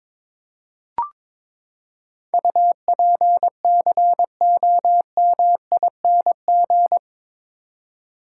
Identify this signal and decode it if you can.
Morse code, heard as a keyed tone, about 22 wpm, 700 Hz. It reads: UPCOMING
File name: practice / 22wpm / Eight Letter Words - 22wpm